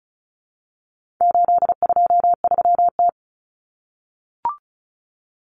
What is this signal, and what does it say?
Morse code, heard as a keyed tone, about 34 wpm, 700 Hz. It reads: 823T
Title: Common Cut Numbers - 4-Digit 35wpm